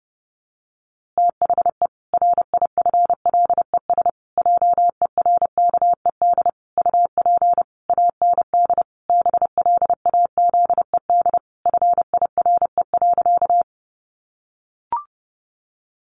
Morse code, heard as a keyed tone, about 30 wpm, 700 Hz. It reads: THE RIFLES JERKED UP AND BLAZED FIRE.
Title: A Sound Of Thunder 30wpm